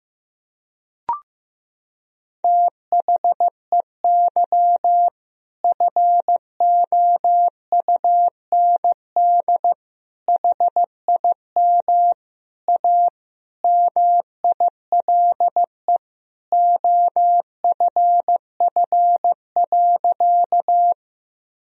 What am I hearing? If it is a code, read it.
Morse code, heard as a keyed tone, about 15 wpm, 700 Hz. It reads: THEY FOUND HIM A MILE OFF.
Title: Sentences from Top 1000 Words - Review Part 1 - 15wpm